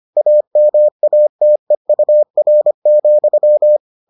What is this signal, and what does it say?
Morse code, heard as a keyed tone, about 25 wpm, 600 Hz. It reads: AMATEUR,